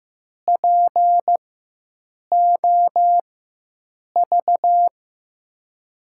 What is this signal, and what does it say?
Morse code, heard as a keyed tone, about 15 wpm, spaced as for 7 wpm, 700 Hz. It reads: POV
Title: USA Call Signs - 6 Characters - 4x Character Spacing - 15wpm